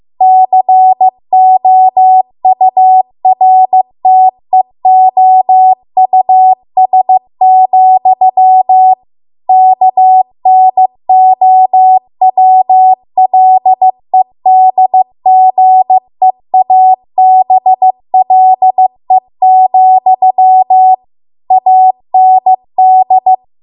Morse code, heard as a keyed tone, about 15 wpm, 750 Hz. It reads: COURTEOUS, KNOWLEDGEABLE, AND
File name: Archive / 15 WPM / 15WPM